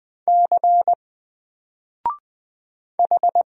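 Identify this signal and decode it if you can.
Morse code, heard as a keyed tone, about 20 wpm, 700 Hz. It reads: C H
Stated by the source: Single letter and number Morse drill at 20 wpm, from N4LCD.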